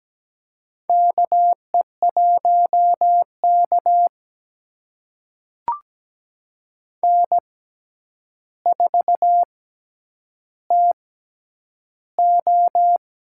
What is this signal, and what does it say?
Morse code, heard as a keyed tone, about 17 wpm, spaced as for 9 wpm, 700 Hz. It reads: KE1K N4TO
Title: USA Call Signs - 4 Characters - 6x Character Spacing - 17wpm